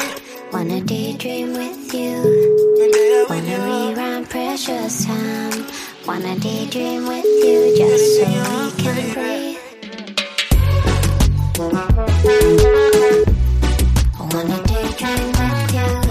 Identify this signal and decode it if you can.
Ringback tone (Europe, 425 Hz), with noise.